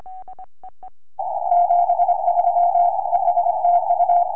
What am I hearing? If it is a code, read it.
Morse code, heard as a keyed tone, about 25 wpm, 690 Hz, with radio noise. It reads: 73 ES TU
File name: ok7gu 4m